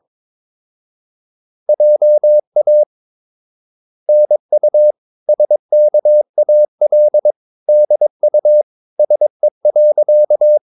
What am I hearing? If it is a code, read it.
Morse code, heard as a keyed tone, about 22 wpm, 600 Hz. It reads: JA NU SKAL DU SE.
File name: Ferie QRS 21wpm